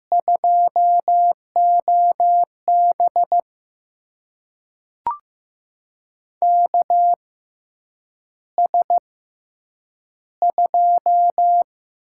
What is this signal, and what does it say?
Morse code, heard as a keyed tone, about 15 wpm, spaced as for 8 wpm, 700 Hz. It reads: 2OB KS2
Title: USA Call Signs - 4 Characters - 6x Character Spacing - 15wpm